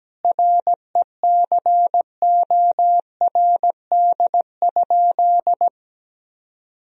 Morse code, heard as a keyed tone, about 17 wpm, 700 Hz. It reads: RECORD?
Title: Sentences from Top 1000 Words - Review Part 4 - 17wpm